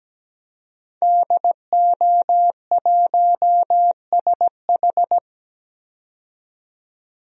Morse code, heard as a keyed tone, about 17 wpm, 700 Hz. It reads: DO1SH
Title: DXCC-100 Call Signs - 5 Characters - 17wpm